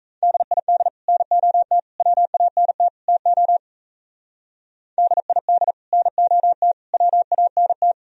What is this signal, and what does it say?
Morse code, heard as a keyed tone, about 33 wpm, 700 Hz. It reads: DID NOT WANT TO DID NOT WANT
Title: Speed-Racing - Sets of 4 Words 28wpm